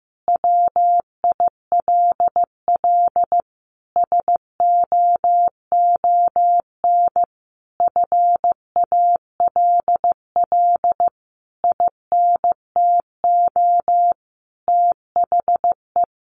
Morse code, heard as a keyed tone, about 15 wpm, 700 Hz. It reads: WILL SOON FALL INTO THE
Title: Sentences from Top 1300 Words 15wpm